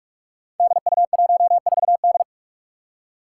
Morse code, heard as a keyed tone, about 46 wpm, 700 Hz. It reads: DU14D
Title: Five Challenging Letter-Number Combinations 45wpm